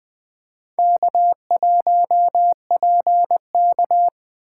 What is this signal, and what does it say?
Morse code, heard as a keyed tone, about 20 wpm, 700 Hz. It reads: K1PK
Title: USA Call Signs - 4 Characters - 3x Character Spacing - 20wpm